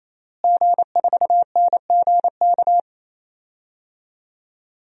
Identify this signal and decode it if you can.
Morse code, heard as a keyed tone, about 28 wpm, 700 Hz. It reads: G4NGK
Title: DXCC-100 Call Signs - 5 Characters - Repeated 3x - 28wpm